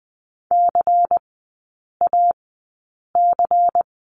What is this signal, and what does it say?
Morse code, heard as a keyed tone, about 20 wpm, spaced as for 8 wpm, 690 Hz. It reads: CAC
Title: 093 Lesson 16e - Letter c - Three Characters 20wpm